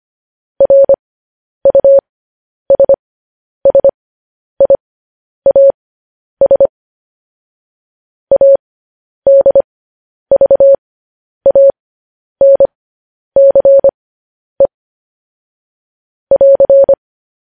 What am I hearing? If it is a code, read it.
Morse code, heard as a keyed tone, about 25 wpm, spaced as for 10 wpm, 550 Hz. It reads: RUSSIAS ADVANCE <AR>
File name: podcast / News headlines-1x-25x10